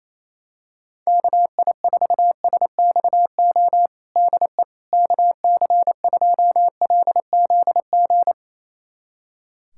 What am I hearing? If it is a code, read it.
Morse code, heard as a keyed tone, about 27 wpm, 700 Hz. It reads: KI4SXO DE KC2LZG
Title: USA Callsign DE Callsign 28wpm